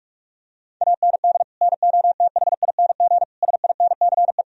Morse code, heard as a keyed tone, about 47 wpm, 700 Hz. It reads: AND NOTHING SINCE
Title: Sentences from Top 1000 Words - Review Part 1 - 45wpm